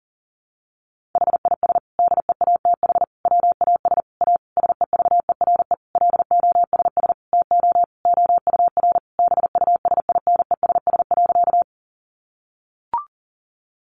Morse code, heard as a keyed tone, about 40 wpm, 700 Hz. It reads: HIS DEATH WAS A SEVERE LOSS TO OUR BUSINESS.
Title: Sentences from Top 1400 Words 40wpm